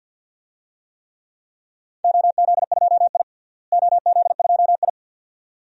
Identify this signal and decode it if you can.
Morse code, heard as a keyed tone, about 48 wpm, 700 Hz. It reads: OZ2I OZ2I
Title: DXCC-100 Call Signs - 4 Characters - Repeated 2x - 50wpm